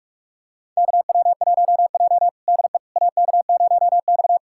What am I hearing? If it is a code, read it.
Morse code, heard as a keyed tone, about 42 wpm, 700 Hz. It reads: KW1J DE AK0X
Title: USA Callsign DE Callsign - 4 Characters - 45wpm